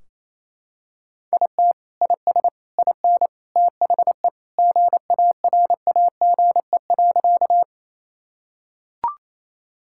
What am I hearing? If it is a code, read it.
Morse code, heard as a keyed tone, about 28 wpm, 700 Hz. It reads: IT IS IN THE GARAGE.
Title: Sentences from Top 1600 Words 28wpm